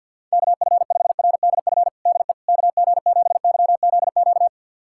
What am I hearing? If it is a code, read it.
Morse code, heard as a keyed tone, about 52 wpm, 700 Hz. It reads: KF5RDF DE KG7YZQ
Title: USA Callsign DE Callsign - 6 Characters - 50wpm